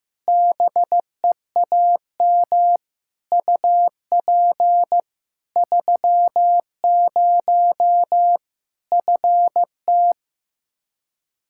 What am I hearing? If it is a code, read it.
Morse code, heard as a keyed tone, about 15 wpm, 700 Hz. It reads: BEAM UP 30 FT
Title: Practice QSO - Exchange 2 - Info - 15wpm